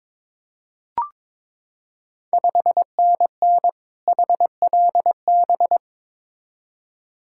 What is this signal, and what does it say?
Morse code, heard as a keyed tone, about 22 wpm, 700 Hz. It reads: 5NN HLB
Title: Ontario QSO Party Exchange 22wpm